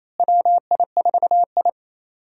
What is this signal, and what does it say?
Morse code, heard as a keyed tone, about 28 wpm, 700 Hz. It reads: WI4I
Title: USA Call Signs - 4 Characters - Repeated 3x - 28wpm